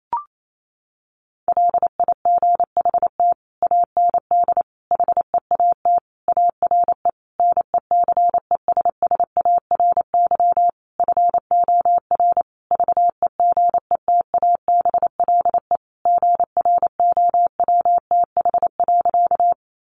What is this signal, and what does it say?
Morse code, heard as a keyed tone, about 28 wpm, 700 Hz. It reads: LIGHT AND HEAT ARE NECESSARY FOR VEGETABLE GROWTH.
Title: Sentences from Top 1300 Words 28wpm